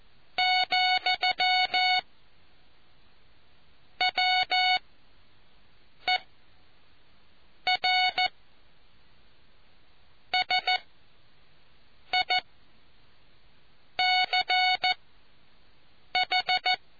Morse code, heard as a keyed tone, about 14 wpm, spaced as for 5 wpm, 740 Hz. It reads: , WER SICH